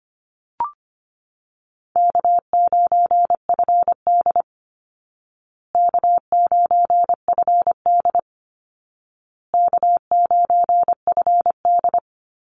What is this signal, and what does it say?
Morse code, heard as a keyed tone, about 25 wpm, 700 Hz. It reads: K9FD K9FD K9FD
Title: Call Signs Repeated 2x - 25wpm